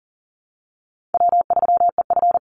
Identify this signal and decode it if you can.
Morse code, heard as a keyed tone, about 40 wpm, 700 Hz. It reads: W3EF
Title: Call Signs Repeated 3x - 40wpm